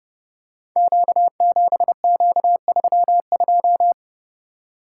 Morse code, heard as a keyed tone, about 30 wpm, 700 Hz. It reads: Q7Q32
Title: Code Groups - 5 Letter:Numbers - 30wpm